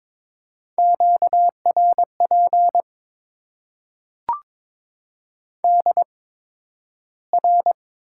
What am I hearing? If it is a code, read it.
Morse code, heard as a keyed tone, about 22 wpm, spaced as for 12 wpm, 700 Hz. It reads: QRP DR
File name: Top 90 QSO Elements - 8x Character Spacing - 22wpm